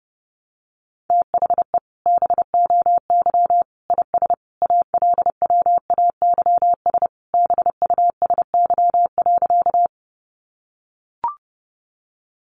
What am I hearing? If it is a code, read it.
Morse code, heard as a keyed tone, about 30 wpm, 700 Hz. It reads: THE BOY IS ALWAYS BUSY.
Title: Additional Easy Phrases 30wpm